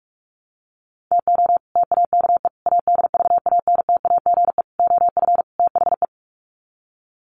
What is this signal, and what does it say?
Morse code, heard as a keyed tone, about 44 wpm, 700 Hz. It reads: TO TAKE ADVANTAGE OF THE
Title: Sets of 5 Words 45wpm